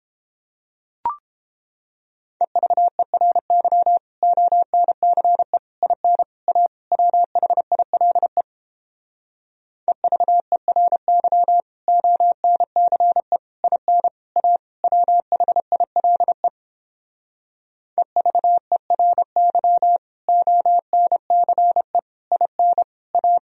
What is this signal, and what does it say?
Morse code, heard as a keyed tone, about 29 wpm, 700 Hz. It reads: EVERY ONCE IN A WHILE EVERY ONCE IN A WHILE EVERY ONCE IN A